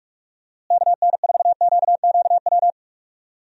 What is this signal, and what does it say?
Morse code, heard as a keyed tone, about 46 wpm, 700 Hz. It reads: KN4QQW
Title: USA Call Signs - 6 Characters - Repeated 2x - 45wpm